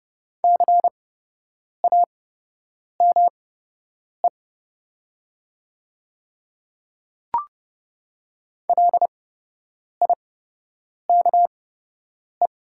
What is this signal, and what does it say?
Morse code, heard as a keyed tone, about 30 wpm, spaced as for 8 wpm, 700 Hz. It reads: CAME LIKE
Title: 8x Character Spacing - Top 300 Words - 30wpm